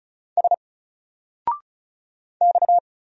Morse code, heard as a keyed tone, about 36 wpm, 700 Hz. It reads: S X
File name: Single-Letter-Number-35wpm